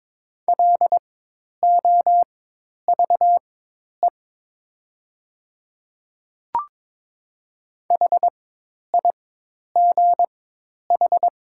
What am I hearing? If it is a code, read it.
Morse code, heard as a keyed tone, about 23 wpm, spaced as for 10 wpm, 700 Hz. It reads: LOVE HIGH
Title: 4x Character Spacing - Top 500 Words - 22wpm